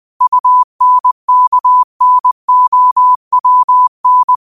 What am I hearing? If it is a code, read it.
Morse code, heard as a keyed tone, about 20 wpm, 1000 Hz. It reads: UNKNOWN
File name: qotdfiles / 20wpm